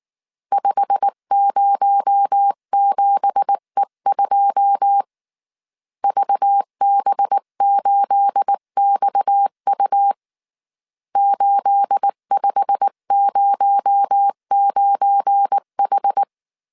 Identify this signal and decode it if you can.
Morse code, heard as a keyed tone, about 19 wpm, 800 Hz. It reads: 507E2 VB8XU 8509H